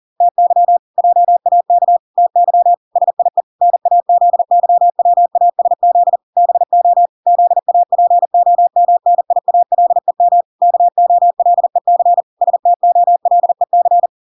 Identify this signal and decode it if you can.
Morse code, heard as a keyed tone, about 40 wpm, 700 Hz. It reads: TY JAK TY SIE NAZYWASZ BO ZAPOMNIALEM KOLEC STOLEC